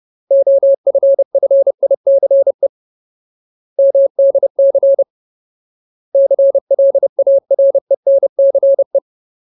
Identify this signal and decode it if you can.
Morse code, heard as a keyed tone, about 30 wpm, 550 Hz. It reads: OFFICE MDC CLARENCE
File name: Ultimate Word List / 30WPM Ultimate